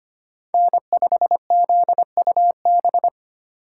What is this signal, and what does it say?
Morse code, heard as a keyed tone, about 25 wpm, 700 Hz. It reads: N5ZUB
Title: USA Call Signs - 5 Characters - 2x Character Spacing - 25wpm